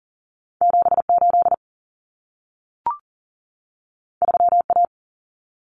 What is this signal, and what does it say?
Morse code, heard as a keyed tone, about 40 wpm, 700 Hz. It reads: 78 3A
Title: Common Cut Numbers - 2-Digit 40wpm